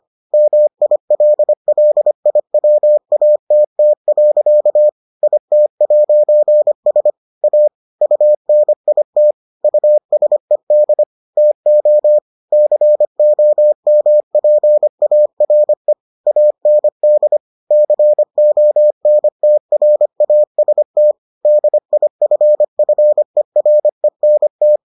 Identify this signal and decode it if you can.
Morse code, heard as a keyed tone, about 25 wpm, 600 Hz. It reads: MILLIWATT. IT'S A UNIT USED TO COMPARE AND CONTRAST DIFFERENT